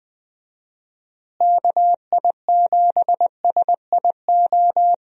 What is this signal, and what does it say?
Morse code, heard as a keyed tone, about 20 wpm, 700 Hz. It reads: KI7SIO
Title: USA Call Signs - 6 Characters - Repeated 2x - 20wpm